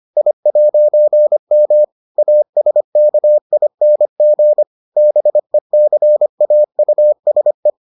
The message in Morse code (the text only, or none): I'M ASKING BECAUSE